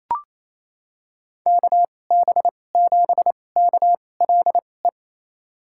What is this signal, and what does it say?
Morse code, heard as a keyed tone, about 28 wpm, spaced as for 20 wpm, 700 Hz. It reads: KB7KLE
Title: USA Call Signs - 6 Characters - 2x Character Spacing - 28wpm